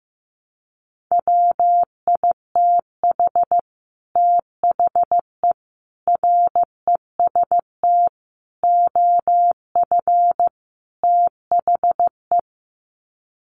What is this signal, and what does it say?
Morse code, heard as a keyed tone, about 15 wpm, 700 Hz. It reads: WITH THE REST OF THE
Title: Sets of 5 Words 15wpm